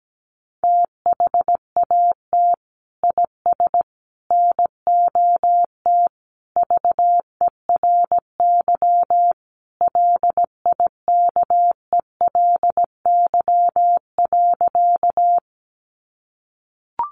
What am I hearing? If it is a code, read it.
Morse code, heard as a keyed tone, about 17 wpm, 700 Hz. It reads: THAT IS NOT VERY LIKELY.
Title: Sentences from Top 600 Words 17wpm